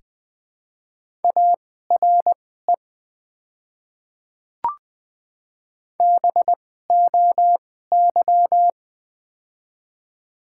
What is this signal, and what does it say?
Morse code, heard as a keyed tone, about 20 wpm, spaced as for 14 wpm, 700 Hz. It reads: ARE BOY
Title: 2x Character Spacing - Top 300 Words - 20wpm